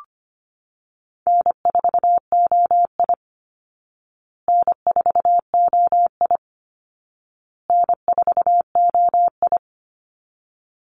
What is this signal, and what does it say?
Morse code, heard as a keyed tone, about 25 wpm, 700 Hz. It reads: N4OI N4OI N4OI
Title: Easy Call Signs Repeated 3x - 25wpm